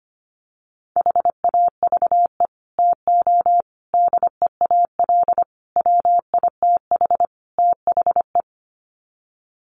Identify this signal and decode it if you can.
Morse code, heard as a keyed tone, about 25 wpm, 700 Hz. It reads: HAVE TO DEAL WITH THE
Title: Sets of 5 Words Encore2 25wpm